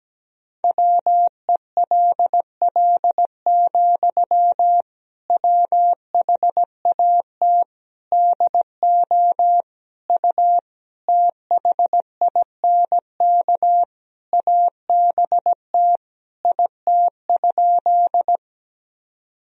Morse code, heard as a keyed tone, about 17 wpm, 700 Hz. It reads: WELL, WHAT DO U THINK ABT IT?
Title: Sentences from Top 200 Words and Abbreviations 17wpm